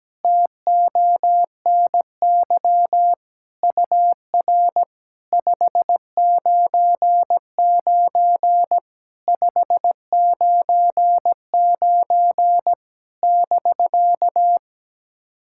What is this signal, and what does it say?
Morse code, heard as a keyed tone, about 17 wpm, 700 Hz. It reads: TONY UR 599 599 <BK>
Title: SOTA QSO Practice 17wpm